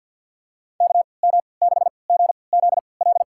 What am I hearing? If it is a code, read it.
Morse code, heard as a keyed tone, about 52 wpm, spaced as for 28 wpm, 700 Hz. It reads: KM6GZL
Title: USA Call Signs - 6 Characters - 3x Character Spacing - 50wpm